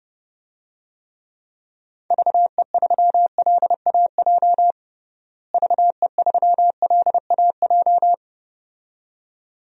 Morse code, heard as a keyed tone, about 30 wpm, 700 Hz. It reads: VE3LAJ VE3LAJ
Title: DXCC-100 Call Signs - 6 Characters - Repeated 2x - 30wpm